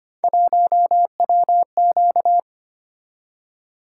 Morse code, heard as a keyed tone, about 26 wpm, 700 Hz. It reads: 1WQ
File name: Three Letter-Number Combinations - 25wpm